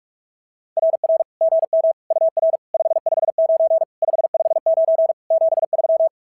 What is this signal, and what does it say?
Morse code, heard as a keyed tone, about 46 wpm, 650 Hz. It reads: RR GM UR 559 559 73
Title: SOTA QSO Practice 45wpm